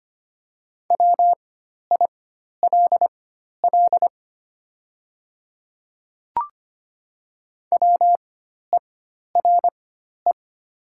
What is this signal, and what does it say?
Morse code, heard as a keyed tone, about 25 wpm, spaced as for 12 wpm, 700 Hz. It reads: WILL WERE
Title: 4x Character Spacing - Top 100 Words - 25wpm